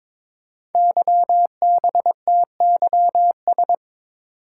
Morse code, heard as a keyed tone, about 23 wpm, 690 Hz. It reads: YBTYS
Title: Code Groups - 5 Letters - 22wpm